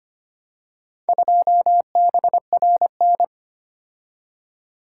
Morse code, heard as a keyed tone, about 25 wpm, 700 Hz. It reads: 2BRN
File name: practice / 25wpm / Four Letter-Number Combinations - 25wpm